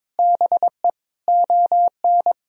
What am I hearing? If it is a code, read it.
Morse code, heard as a keyed tone, about 22 wpm, 700 Hz. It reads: BE ON
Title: Sentences from Top 1000 Words - Review Part 6 - 22wpm